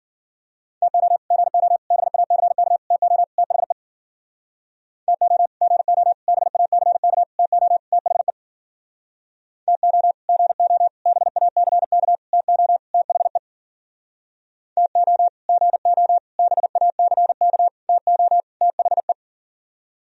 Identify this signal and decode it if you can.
Morse code, heard as a keyed tone, about 51 wpm, 700 Hz. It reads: TO GO BACK TO THE TO GO BACK TO THE TO GO BACK TO THE TO GO BACK TO THE